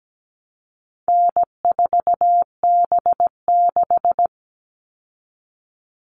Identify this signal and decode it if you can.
Morse code, heard as a keyed tone, about 17 wpm, 700 Hz. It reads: N4B6
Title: All Cut Numbers - 4-Digit 17wpm